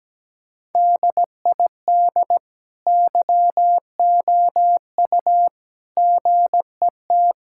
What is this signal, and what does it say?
Morse code, heard as a keyed tone, about 17 wpm, 700 Hz. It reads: DID YOU GET
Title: Sentences from Top 1000 Words - Review Part 5 - 17wpm